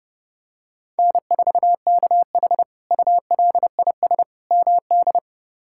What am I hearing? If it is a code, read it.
Morse code, heard as a keyed tone, about 32 wpm, 700 Hz. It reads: N4KH ULIS MD
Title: K1USN SST QSO Practice - 30wpm